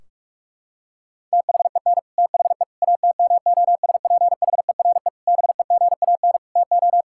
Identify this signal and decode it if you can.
Morse code, heard as a keyed tone, about 41 wpm, 700 Hz. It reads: THEN THE ATMOSPHERE BEGAN TO
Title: Sentences from Top 1600 Words 45wpm